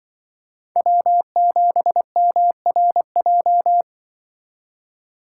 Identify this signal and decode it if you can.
Morse code, heard as a keyed tone, about 25 wpm, 700 Hz. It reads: W7MRJ
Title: Speed Racing - USA Call Signs 20wpm